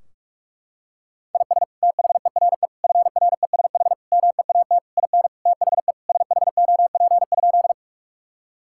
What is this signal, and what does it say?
Morse code, heard as a keyed tone, about 46 wpm, 700 Hz. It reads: IS THERE FRESH MEAT IN THE SHOP?
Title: Sentences from Top 1600 Words 45wpm